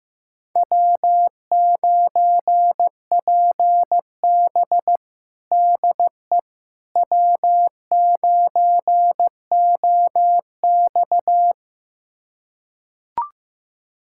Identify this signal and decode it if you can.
Morse code, heard as a keyed tone, about 15 wpm, 700 Hz. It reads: W9PB DE W9OX
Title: USA Callsign DE Callsign - 4 Characters - 15wpm